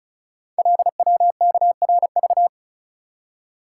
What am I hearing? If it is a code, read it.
Morse code, heard as a keyed tone, about 36 wpm, 700 Hz. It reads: LWKRV
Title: Code Groups - 5 Letters - 35wpm